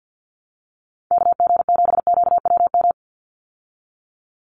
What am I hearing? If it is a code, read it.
Morse code, heard as a keyed tone, about 50 wpm, 700 Hz. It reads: KG7QWM
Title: USA Call Signs 50wpm